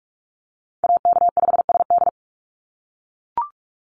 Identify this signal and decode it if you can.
Morse code, heard as a keyed tone, about 44 wpm, 690 Hz. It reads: AK5SD
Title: NA SOTA Chaser Callsigns 45wpm